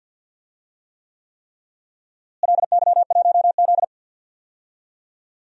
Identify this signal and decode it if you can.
Morse code, heard as a keyed tone, about 52 wpm, 700 Hz. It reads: LY1Z
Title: DXCC-100 Call Signs - 4 Characters - Repeated 3x - 50wpm